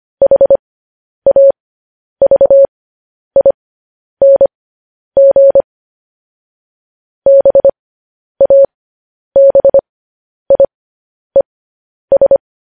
Morse code, heard as a keyed tone, about 25 wpm, spaced as for 10 wpm, 550 Hz. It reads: HAVING BABIES
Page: news headlines in morse code at 10 wpm